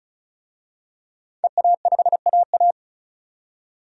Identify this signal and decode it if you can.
Morse code, heard as a keyed tone, about 34 wpm, 700 Hz. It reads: EA5AA
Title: DXCC-100 Call Signs - 5 Characters - 35wpm